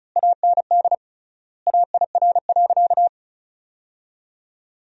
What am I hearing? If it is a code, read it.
Morse code, heard as a keyed tone, about 36 wpm, 700 Hz. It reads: AND AIR.
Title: 3x Word-Spacing - Sentences from Top 400 Words - 35wpm